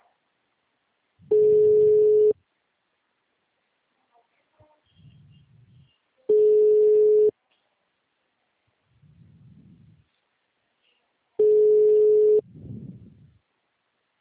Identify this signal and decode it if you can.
Ringback tone (Europe, 425 Hz), with speech.